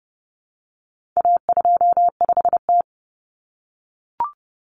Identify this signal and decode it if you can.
Morse code, heard as a keyed tone, about 30 wpm, 700 Hz. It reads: A25T
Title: Common Cut Numbers - 4-Digit 30wpm